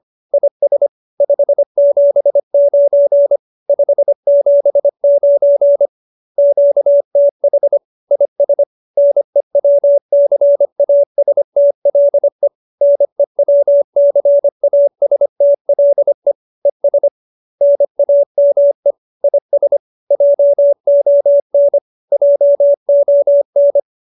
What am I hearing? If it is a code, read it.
Morse code, heard as a keyed tone, about 25 wpm, 570 Hz. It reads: IS 579 579 QTH IS NEWCASTLE NEWCASTLE ES NAME IS JON JON